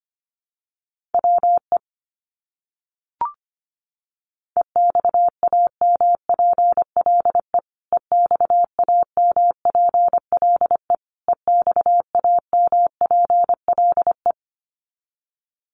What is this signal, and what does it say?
Morse code, heard as a keyed tone, about 25 wpm, 700 Hz. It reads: WE EXAMPLE EXAMPLE EXAMPLE
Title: Top 300 Words Repeated 25wpm